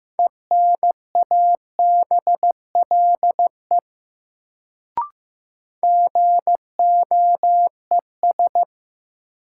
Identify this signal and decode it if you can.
Morse code, heard as a keyed tone, about 15 wpm, 700 Hz. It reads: ENABLE GOES
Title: Sentences from Top 1500 Words - New Word Drill Alt - 15wpm